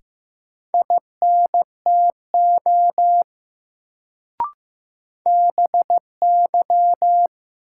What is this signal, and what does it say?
Morse code, heard as a keyed tone, about 15 wpm, 700 Hz. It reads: INTO BY
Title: Sets of 2 Words - New Word Drill - 15wpm